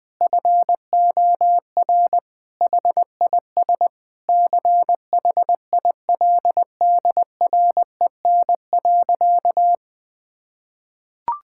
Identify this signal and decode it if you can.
Morse code, heard as a keyed tone, about 21 wpm, 700 Hz. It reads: FOR HIS CHILDREN.
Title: Sentences from Top 1600 Words 20wpm